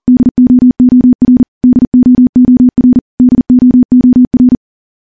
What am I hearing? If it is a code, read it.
Morse code, heard as a keyed tone, about 41 wpm, 250 Hz. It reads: DOOR DOOR DOOR
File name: morse / morse-training-40WPM